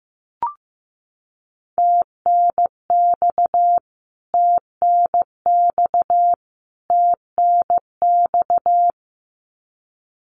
Morse code, heard as a keyed tone, about 15 wpm, 700 Hz. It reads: TNX TNX TNX
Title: Abbreviations Repeated 3x - 15wpm